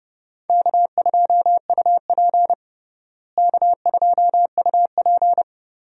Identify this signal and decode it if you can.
Morse code, heard as a keyed tone, about 30 wpm, 700 Hz. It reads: K2UP K2UP
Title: USA Call Signs - 4 Characters - Repeated 3x - 30wpm